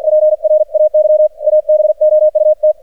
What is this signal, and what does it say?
Morse code, heard as a keyed tone, about 48 wpm, 610 Hz. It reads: PRAY ABOUT
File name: archived / CRSnet